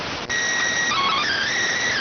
SSTV (slow-scan television), PD 290.